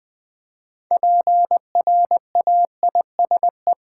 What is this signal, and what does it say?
Morse code, heard as a keyed tone, about 20 wpm, 700 Hz. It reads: PRAISE
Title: Build A Word - 2 Letters At A Time - 6-Letter Words - 20wpm